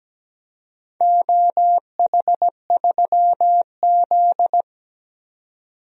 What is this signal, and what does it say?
Morse code, heard as a keyed tone, about 17 wpm, 700 Hz. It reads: OH3Z
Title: DXCC-100 Call Signs - 4 Characters - 17wpm